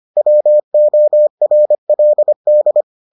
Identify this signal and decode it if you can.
Morse code, heard as a keyed tone, about 25 wpm, 600 Hz. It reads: WORLD